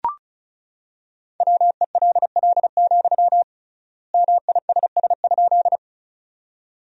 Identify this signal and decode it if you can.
Morse code, heard as a keyed tone, about 36 wpm, 700 Hz. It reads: WELL, MISS?
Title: 3x Word-Spacing - Sentences from Top 500 Words - 35wpm